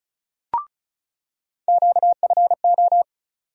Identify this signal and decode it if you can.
Morse code, heard as a keyed tone, about 34 wpm, 700 Hz. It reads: QFO
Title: USA Call Sign Postfixes - 3 Letters 35wpm